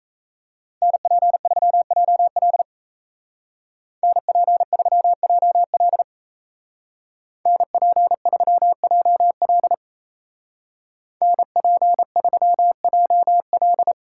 Morse code, heard as a keyed tone, about 34 wpm, 700 Hz. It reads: NP3JL NP3JL NP3JL NP3JL
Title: Speed Racing - USA 2x2 Call Signs 28wpm